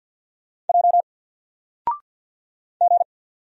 Morse code, heard as a keyed tone, about 49 wpm, 700 Hz. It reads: J G
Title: Single Character (Letter:Number:Punctuation:Prosign) 50wpm